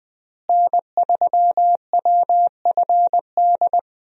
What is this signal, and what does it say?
Morse code, heard as a keyed tone, about 21 wpm, 700 Hz. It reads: N3WFD
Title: USA Call Signs - 5 Characters - Repeated 2x - 20wpm